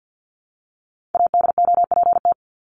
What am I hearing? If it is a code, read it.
Morse code, heard as a keyed tone, about 50 wpm, 700 Hz. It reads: ADOPT
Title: ED Suffix - Top 100 Words with Base Words 50wpm